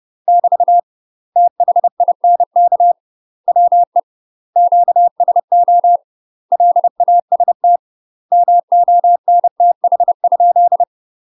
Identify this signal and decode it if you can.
Morse code, heard as a keyed tone, about 30 wpm, 700 Hz. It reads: <BT> THINK WE QSO LAST MONTH?
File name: UK MASTER QSOs TRAINING / 30WPM / UK MASTER QSOs 0044